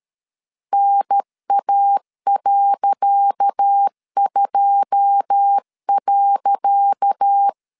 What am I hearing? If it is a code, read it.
Morse code, heard as a keyed tone, about 13 wpm, 800 Hz. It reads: NA.2.
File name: LEZIONI / 10 SIGNS / 10C DOT 12wpm 2451